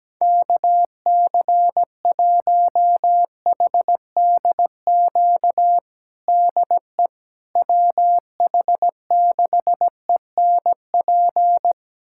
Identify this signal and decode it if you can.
Morse code, heard as a keyed tone, about 17 wpm, 700 Hz. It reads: KC1HDQ DE WH6ENP
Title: USA Callsign DE Callsign - 6 Characters - 17wpm